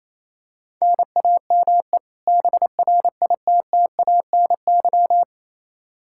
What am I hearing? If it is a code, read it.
Morse code, heard as a keyed tone, about 28 wpm, 700 Hz. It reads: NAME BRITTANY
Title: K1USN SST QSO Practice 28wpm